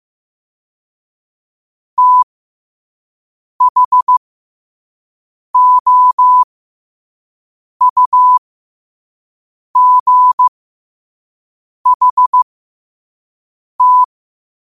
Morse code, heard as a keyed tone, about 15 wpm, spaced as for 5 wpm, 1000 Hz. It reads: THOUGHT
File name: qotdfiles / 5wpm